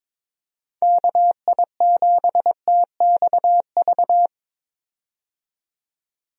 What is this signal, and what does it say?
Morse code, heard as a keyed tone, about 21 wpm, 700 Hz. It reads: KI7TXV
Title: USA Call Signs - 6 Characters - 22wpm